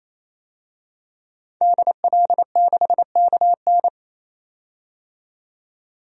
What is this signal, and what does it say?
Morse code, heard as a keyed tone, about 28 wpm, 700 Hz. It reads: DL6KN
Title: DXCC-100 Call Signs - 5 Characters - 28wpm